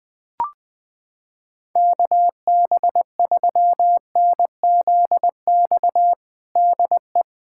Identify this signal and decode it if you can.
Morse code, heard as a keyed tone, about 20 wpm, 700 Hz. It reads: KB3NZX DE
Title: USA Callsign DE Callsign 20wpm